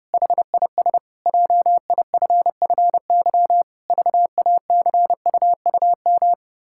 Morse code, heard as a keyed tone, about 29 wpm, 700 Hz. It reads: HIS JIFFY VACUUM
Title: Pangram Challenge v2 - 30wpm